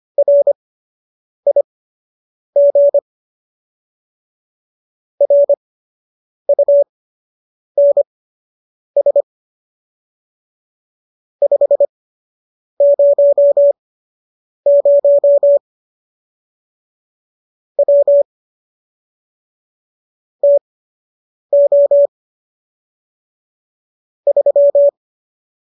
Morse code, heard as a keyed tone, about 25 wpm, spaced as for 8 wpm, 570 Hz. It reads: RIG RUNS 500 W TO 3